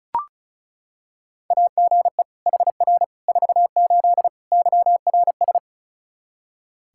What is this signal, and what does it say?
Morse code, heard as a keyed tone, about 34 wpm, 700 Hz. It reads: AGE HR 48 YRS
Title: K1USN SST QSO Practice 35wpm